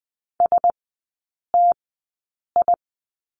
Morse code, heard as a keyed tone, about 20 wpm, spaced as for 8 wpm, 690 Hz. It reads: STI
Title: 081 Lesson 14e - Number 2 - Three Characters 20wpm